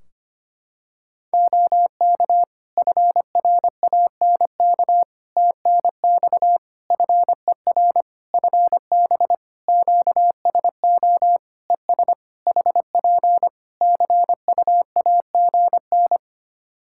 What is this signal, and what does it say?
Morse code, heard as a keyed tone, about 25 wpm, 700 Hz. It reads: OK FRANK TNX FER FB QSO ES HP CUAGN
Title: Practice QSO - Exchange 4 - Ending - 25wpm